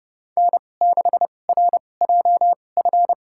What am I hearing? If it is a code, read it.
Morse code, heard as a keyed tone, about 30 wpm, spaced as for 22 wpm, 700 Hz. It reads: N6RJF
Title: USA Call Signs - 2x Character Spacing - 30wpm